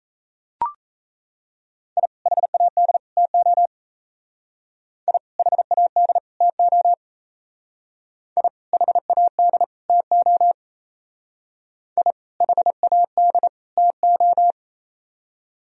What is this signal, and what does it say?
Morse code, heard as a keyed tone, about 35 wpm, 700 Hz. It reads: I HAD TO I HAD TO I HAD TO I HAD TO